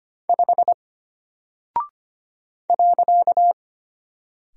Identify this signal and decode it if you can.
Morse code, heard as a keyed tone, about 25 wpm, 700 Hz. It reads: 5 .
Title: Single Character (Letter:Number:Punctuation:Prosign) Alt 25wpm